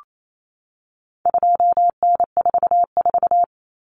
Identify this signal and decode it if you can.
Morse code, heard as a keyed tone, about 28 wpm, 700 Hz. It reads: 2N44
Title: Common Cut Numbers - 4-Digit 28wpm